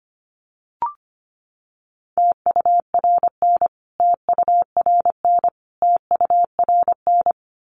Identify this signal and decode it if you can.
Morse code, heard as a keyed tone, about 25 wpm, 700 Hz. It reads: TURN TURN TURN
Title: Top 500 Words Repeated 25wpm